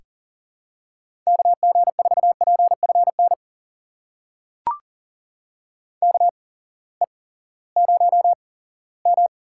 Morse code, heard as a keyed tone, about 39 wpm, spaced as for 20 wpm, 700 Hz. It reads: KG4PFN KE0M
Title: USA Call Signs - 6 Characters - 8x Character Spacing - 40wpm